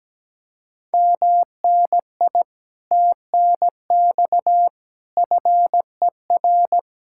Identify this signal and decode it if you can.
Morse code, heard as a keyed tone, about 17 wpm, 700 Hz. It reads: MNI TNX FER
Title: Speed-Racing Common QSO Phrases - 17wpm